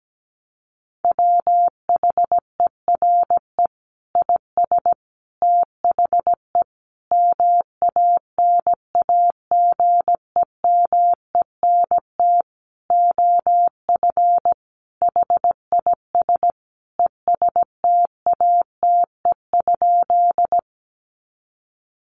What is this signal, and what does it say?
Morse code, heard as a keyed tone, about 17 wpm, 700 Hz. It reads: WHERE IS THE MANAGEMENT OF HIS ESTATE?
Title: Sentences from Top 1200 Words 17wpm